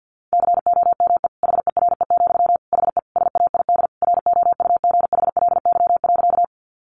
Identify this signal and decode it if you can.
Morse code, heard as a keyed tone, about 50 wpm, 700 Hz. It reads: COME HERE, HE SAID ROUGHLY.
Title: Sentences from Top 1500 Words 50wpm